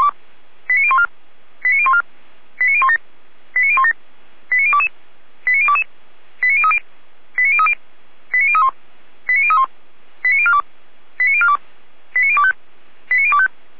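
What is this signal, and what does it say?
Five-tone selective call (ZVEI), heard as quick runs of five tones, tones of about 70 ms. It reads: 89014 89014 89017 89017 89022 89022 89030 89030 89031 89031 89042 89042 89025 89025
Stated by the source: ZVEI five-tone selective call received by SDR.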